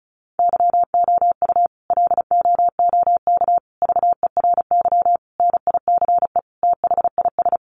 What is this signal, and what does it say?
Morse code, heard as a keyed tone, about 34 wpm, 700 Hz. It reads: YOU LOOK VERY NICE THIS
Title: Sentences from Top 1500 Words 35wpm